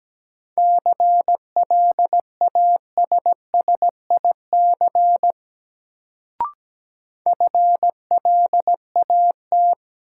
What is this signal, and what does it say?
Morse code, heard as a keyed tone, about 17 wpm, 700 Hz. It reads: CLASSIC FLAT
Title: Sentences from Top 1000 Words - New Word Drill - 17wpm